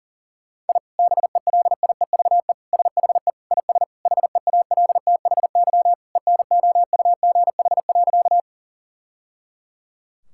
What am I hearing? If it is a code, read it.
Morse code, heard as a keyed tone, about 40 wpm, 700 Hz. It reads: I BELIEVE SHE IS HEALTHY ENOUGH.
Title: Sentences from Top 1000 Words - Review Part 5 - 40wpm